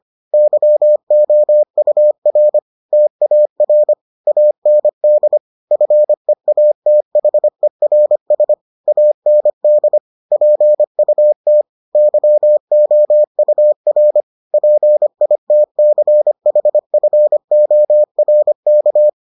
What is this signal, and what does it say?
Morse code, heard as a keyed tone, about 25 wpm, 600 Hz. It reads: YOUR TAR AND FEATHERS AND PUT YOUR PITCHFORK